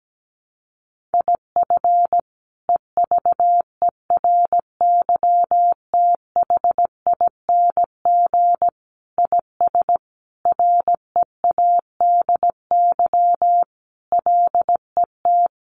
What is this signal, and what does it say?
Morse code, heard as a keyed tone, about 17 wpm, 700 Hz. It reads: IF EVERYTHING IS READY LET
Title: Sentences from Top 700 Words 17wpm